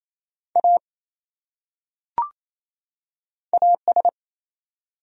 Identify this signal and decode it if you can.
Morse code, heard as a keyed tone, about 28 wpm, 700 Hz. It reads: A AS
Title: Top 100 Words 28wpm